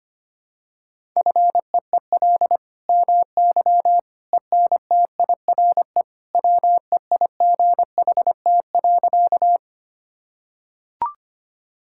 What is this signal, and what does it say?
Morse code, heard as a keyed tone, about 25 wpm, 700 Hz. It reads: FEEL MY ENTIRE WEIGHT.
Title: Sentences from Top 1000 Words - Review Part 5 - 25wpm